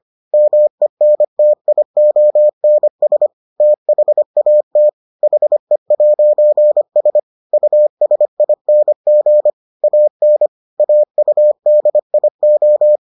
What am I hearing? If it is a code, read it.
Morse code, heard as a keyed tone, about 25 wpm, 600 Hz. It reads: MENTIONS THAT HE'S USING AN AUDIO